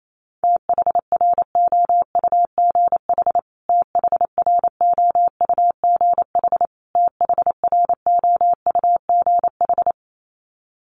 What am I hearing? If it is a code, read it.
Morse code, heard as a keyed tone, about 28 wpm, 700 Hz. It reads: THROUGH THROUGH THROUGH
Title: Top 500 Words Repeated 28wpm